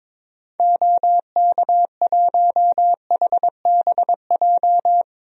Morse code, heard as a keyed tone, about 22 wpm, 700 Hz. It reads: OK1HBJ
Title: DXCC-100 Call Signs - 6 Characters - Repeated 3x - 22wpm